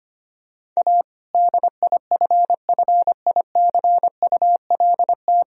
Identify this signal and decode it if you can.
Morse code, heard as a keyed tone, about 25 wpm, 700 Hz. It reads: A DIFFICULT
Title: Sentences from Top 1000 Words - Review Part 2 - 25wpm